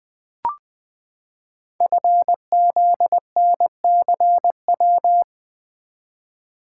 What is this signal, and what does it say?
Morse code, heard as a keyed tone, about 20 wpm, 700 Hz. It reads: FZNCW
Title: Code Groups - 5 Letters - 20wpm